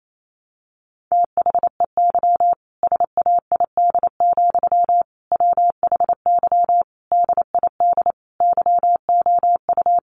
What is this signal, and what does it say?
Morse code, heard as a keyed tone, about 28 wpm, 700 Hz. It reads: THEY SAID, WHY DID YOU
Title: Sentences from Top 1200 Words 28wpm